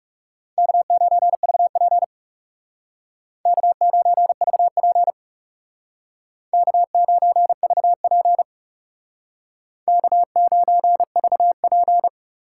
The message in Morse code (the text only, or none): K9VP K9VP K9VP K9VP